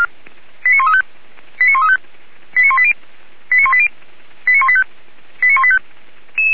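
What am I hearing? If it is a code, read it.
Five-tone selective call (ZVEI), heard as quick runs of five tones, tones of about 70 ms. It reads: 89135 89135 89180 89180 89185 89185
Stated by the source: Five-tone ZVEI selective call received by SDR, archived by the fire brigade Feuerwehr Orlamünde.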